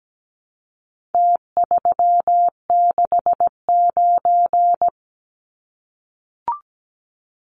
Morse code, heard as a keyed tone, about 17 wpm, 700 Hz. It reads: T369
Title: Numbers - Contest Style - 4-Digit 17wpm